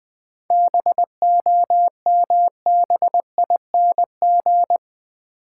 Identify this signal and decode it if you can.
Morse code, heard as a keyed tone, about 20 wpm, 700 Hz. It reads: BOMBING
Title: 7 Letter Words - First 3 Letters Given - 20wpm